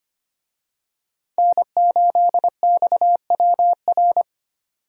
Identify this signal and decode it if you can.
Morse code, heard as a keyed tone, about 25 wpm, 700 Hz. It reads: N8XWR
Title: USA Call Signs - 5 Characters - 25wpm